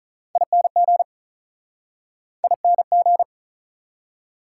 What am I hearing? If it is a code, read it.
Morse code, heard as a keyed tone, about 49 wpm, 700 Hz. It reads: ING ING